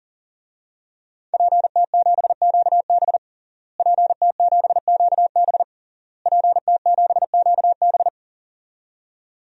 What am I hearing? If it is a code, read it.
Morse code, heard as a keyed tone, about 40 wpm, 700 Hz. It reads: PT7QB PT7QB PT7QB
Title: DXCC-100 Call Signs - 5 Characters - Repeated 3x - 40wpm